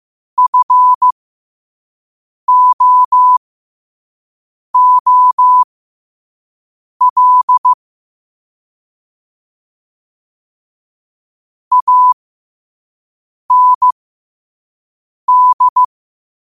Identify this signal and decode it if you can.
Morse code, heard as a keyed tone, about 15 wpm, spaced as for 5 wpm, 1000 Hz. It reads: FOOL AND
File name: qotdfiles / 5wpm